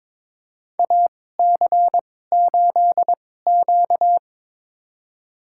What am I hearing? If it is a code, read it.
Morse code, heard as a keyed tone, about 22 wpm, spaced as for 16 wpm, 700 Hz. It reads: AC8Q
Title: USA Call Signs - 4 Characters - 2x Character Spacing - 22wpm